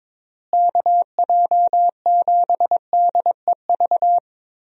Morse code, heard as a keyed tone, about 22 wpm, 700 Hz. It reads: KJ7DEV
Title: USA Call Signs - 6 Characters - Repeated 2x - 22wpm